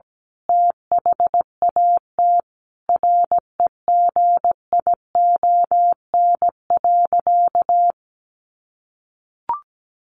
Morse code, heard as a keyed tone, about 17 wpm, 700 Hz. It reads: THAT REGION.
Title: A Sound Of Thunder 17wpm